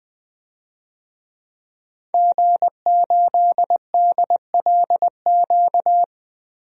Morse code, heard as a keyed tone, about 20 wpm, 700 Hz. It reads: G8DLQ
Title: DXCC-100 Call Signs - 5 Characters - Repeated 2x - 20wpm